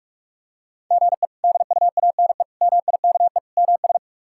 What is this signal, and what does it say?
Morse code, heard as a keyed tone, about 42 wpm, 700 Hz. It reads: GE DUANE MIKE MS
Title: K1USN SST QSO Practice - 45wpm